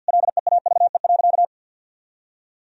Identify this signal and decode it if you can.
Morse code, heard as a keyed tone, about 50 wpm, 700 Hz. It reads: LEAVE.
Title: Sentences from Top 1000 Words - Review Part 4 - 50wpm